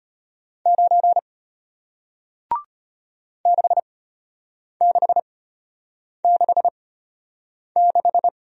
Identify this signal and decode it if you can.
Morse code, heard as a keyed tone, about 34 wpm, spaced as for 8 wpm, 700 Hz. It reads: 9 6666